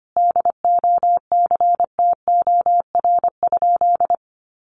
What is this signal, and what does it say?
Morse code, heard as a keyed tone, about 25 wpm, 700 Hz. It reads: DOCTOR?
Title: Sentences from Top 1500 Words 25wpm